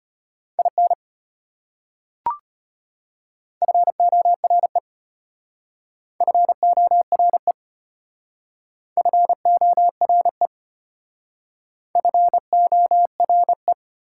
Morse code, heard as a keyed tone, about 32 wpm, 690 Hz. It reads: IN FORE FORE FORE FORE